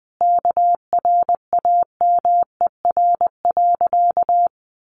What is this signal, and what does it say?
Morse code, heard as a keyed tone, about 19 wpm, 700 Hz. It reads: KRAMER.
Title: A Sound Of Thunder 20wpm